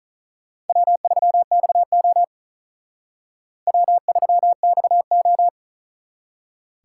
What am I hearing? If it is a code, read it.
Morse code, heard as a keyed tone, about 45 wpm, 700 Hz. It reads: W3XO W3XO